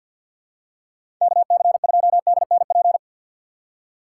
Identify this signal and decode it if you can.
Morse code, heard as a keyed tone, about 52 wpm, 700 Hz. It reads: KC2DNP
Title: USA Call Signs - 2x Character Spacing - 50wpm